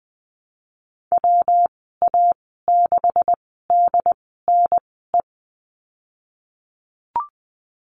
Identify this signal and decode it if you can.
Morse code, heard as a keyed tone, about 20 wpm, spaced as for 15 wpm, 700 Hz. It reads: WA6DNE
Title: 147 Lesson 24g - Letter y - Call Signs 20wpm